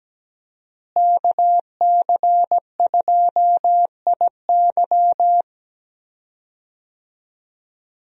Morse code, heard as a keyed tone, about 17 wpm, 700 Hz. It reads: KC2IY
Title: USA Call Signs - 5 Characters - Repeated 3x - 17wpm